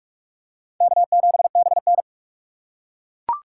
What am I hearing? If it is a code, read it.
Morse code, heard as a keyed tone, about 39 wpm, 700 Hz. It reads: K7BN